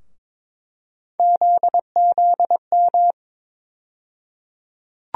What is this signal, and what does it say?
Morse code, heard as a keyed tone, about 22 wpm, 700 Hz. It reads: ZZM
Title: USA Call Sign Postfixes - 3 Letters 22wpm